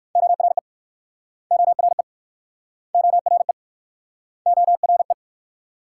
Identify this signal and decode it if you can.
Morse code, heard as a keyed tone, about 64 wpm, 690 Hz. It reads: ORE ORE ORE ORE